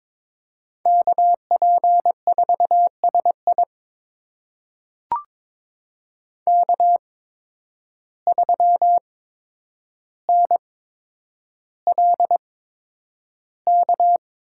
Morse code, heard as a keyed tone, about 22 wpm, spaced as for 9 wpm, 700 Hz. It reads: KP4SI K3NLK
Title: USA Call Signs - 5 Characters - 8x Character Spacing - 22wpm